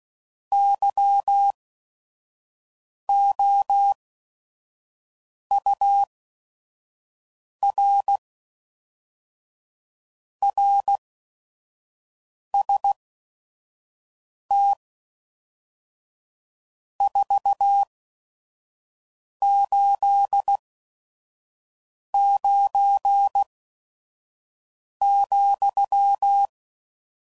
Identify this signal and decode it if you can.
Morse code, heard as a keyed tone, about 16 wpm, spaced as for 5 wpm, 800 Hz. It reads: YOUR RST 489,
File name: CW Code Groups @5WPM / RST's / Your RST's